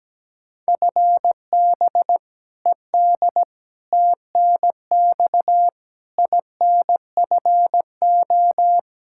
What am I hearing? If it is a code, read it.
Morse code, heard as a keyed tone, about 17 wpm, 700 Hz. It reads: FB ED TNX INFO
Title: K1USN SST QSO Practice 17wpm